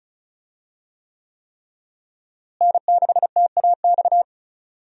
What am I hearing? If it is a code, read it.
Morse code, heard as a keyed tone, about 36 wpm, 700 Hz. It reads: N6TAX
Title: USA Call Signs - 5 Characters - Repeated 3x - 35wpm